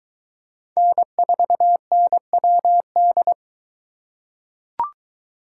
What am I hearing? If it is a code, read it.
Morse code, heard as a keyed tone, about 22 wpm, 700 Hz. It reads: N4NWD